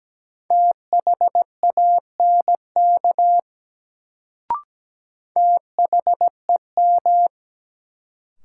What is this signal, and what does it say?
Morse code, heard as a keyed tone, about 17 wpm, 700 Hz. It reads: THANK THEM
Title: Sets of 2 Words - New Word Drill - 17wpm